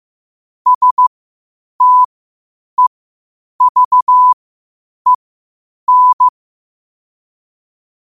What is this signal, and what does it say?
Morse code, heard as a keyed tone, about 15 wpm, spaced as for 8 wpm, 1000 Hz. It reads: STEVEN
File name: 8wpm